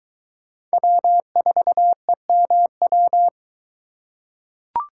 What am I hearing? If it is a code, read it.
Morse code, heard as a keyed tone, about 22 wpm, 700 Hz. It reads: W4EMW